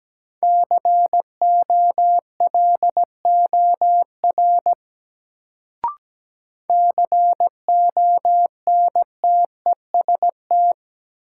Morse code, heard as a keyed tone, about 17 wpm, 700 Hz. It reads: COLOR CONTEST
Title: Sentences from Top 1600 Words - New Word Drill Alt - 17wpm